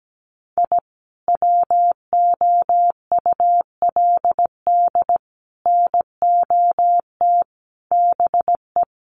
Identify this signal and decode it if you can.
Morse code, heard as a keyed tone, about 17 wpm, 700 Hz. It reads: I WOULD NOT BE
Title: Sets of 5 Words Encore2 17wpm